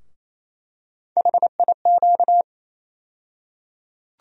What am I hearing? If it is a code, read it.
Morse code, heard as a keyed tone, about 28 wpm, 700 Hz. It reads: HIQ
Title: USA Call Sign Postfixes - 3 Letters 28wpm